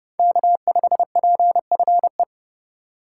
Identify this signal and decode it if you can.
Morse code, heard as a keyed tone, about 30 wpm, 700 Hz. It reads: K5PFE